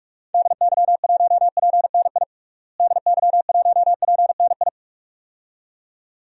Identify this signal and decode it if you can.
Morse code, heard as a keyed tone, about 46 wpm, 700 Hz. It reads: DY1PNI DY1PNI
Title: DXCC-100 Call Signs - 6 Characters - Repeated 2x - 45wpm